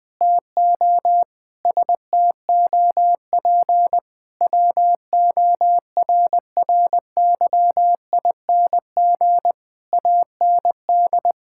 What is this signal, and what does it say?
Morse code, heard as a keyed tone, about 19 wpm, 700 Hz. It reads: TO STOP WORRYING AND
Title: Sentences with 8-Word Phrases - 20wpm